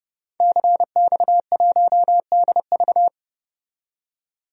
Morse code, heard as a keyed tone, about 30 wpm, 700 Hz. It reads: CX1DV
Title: DXCC-100 Call Signs - 5 Characters - 30wpm